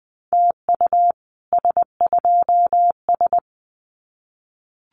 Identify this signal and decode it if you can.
Morse code, heard as a keyed tone, about 20 wpm, 700 Hz. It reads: TU S2S
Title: Common QSO Phrases Encore 20wpm